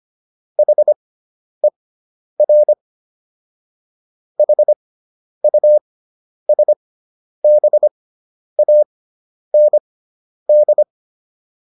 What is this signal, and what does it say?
Morse code, heard as a keyed tone, about 25 wpm, spaced as for 10 wpm, 600 Hz. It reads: HER HUSBAND